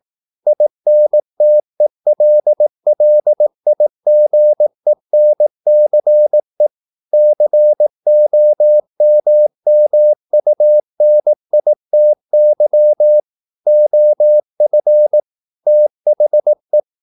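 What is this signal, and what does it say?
Morse code, heard as a keyed tone, about 18 wpm, 600 Hz. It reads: INTELLIGENCE COMMUNITY OF THE
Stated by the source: Morse practice at 18 wpm, English text, from CW med Gnister.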